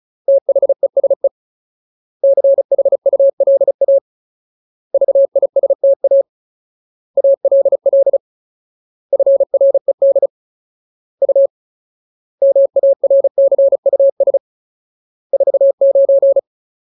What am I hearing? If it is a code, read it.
Morse code, heard as a keyed tone, about 35 wpm, 550 Hz. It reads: THESE CHULA VISTA ALL FRED U MARCUS 49
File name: Ultimate Word List / 35WPM Ultimate